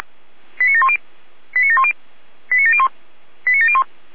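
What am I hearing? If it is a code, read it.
Five-tone selective call (ZVEI), heard as quick runs of five tones, tones of about 70 ms. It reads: 89710 89710 89701 89701
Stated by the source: ZVEI five-tone selective call received by SDR.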